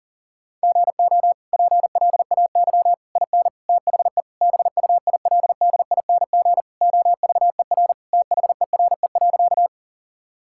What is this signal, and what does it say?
Morse code, heard as a keyed tone, about 40 wpm, 700 Hz. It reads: GO PLAY IN THE BUILDING OVER THERE.
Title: Sentences from Top 1000 Words - Review Part 5 - 40wpm